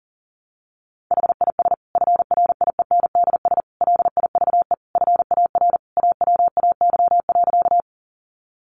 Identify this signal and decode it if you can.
Morse code, heard as a keyed tone, about 40 wpm, 700 Hz. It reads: HIS FRIENDS LIVE FAR AWAY.
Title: Additional Easy Phrases 40wpm